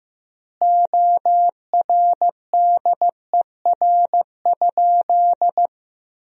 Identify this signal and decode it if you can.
Morse code, heard as a keyed tone, about 15 wpm, 700 Hz. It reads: ORDER?
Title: 3x Word-Spacing - Sentences from Top 400 Words - 15wpm